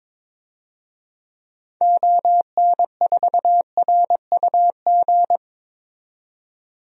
Morse code, heard as a keyed tone, about 22 wpm, 700 Hz. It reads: ON4RUG
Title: DXCC-100 Call Signs - 6 Characters - 22wpm